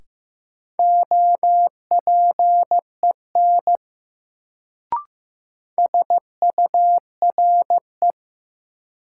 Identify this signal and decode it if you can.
Morse code, heard as a keyed tone, about 15 wpm, 700 Hz. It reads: OPEN SURE
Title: Sentences from Top 300 Words - New Word Drill - 15wpm